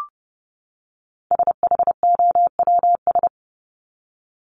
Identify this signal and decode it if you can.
Morse code, heard as a keyed tone, about 30 wpm, 700 Hz. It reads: SHOWS
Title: ES:S Suffix - Top 100 Words with Base Words 30wpm